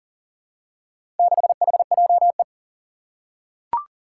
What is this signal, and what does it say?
Morse code, heard as a keyed tone, about 40 wpm, 700 Hz. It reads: 6HJE
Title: Four Challenging Letter-Number Combinations 40wpm